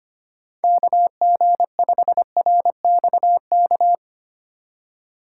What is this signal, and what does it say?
Morse code, heard as a keyed tone, about 25 wpm, 700 Hz. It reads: KG5RXK
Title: USA Call Signs - 6 Characters - 2x Character Spacing - 25wpm